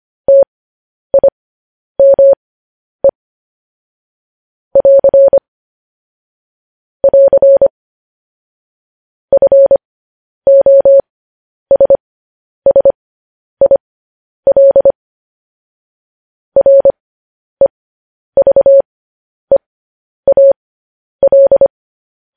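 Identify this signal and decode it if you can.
Morse code, heard as a keyed tone, about 25 wpm, spaced as for 10 wpm, 550 Hz. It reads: TIME <AR> <AR> FOSSIL REVEAL
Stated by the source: Morse code news headlines practice transmission.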